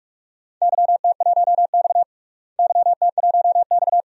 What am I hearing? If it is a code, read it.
Morse code, heard as a keyed tone, about 44 wpm, 700 Hz. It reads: YT1X YT1X
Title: DXCC-100 Call Signs - 4 Characters - Repeated 2x - 45wpm